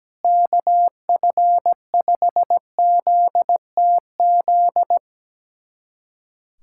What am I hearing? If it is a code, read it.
Morse code, heard as a keyed tone, about 17 wpm, 700 Hz. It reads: KF5ZTZ
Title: USA Callsign DE Callsign 17wpm